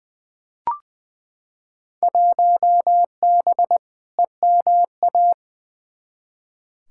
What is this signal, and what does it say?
Morse code, heard as a keyed tone, about 20 wpm, 700 Hz. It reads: 1B EMA
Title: Field Day Exchanges - 20wpm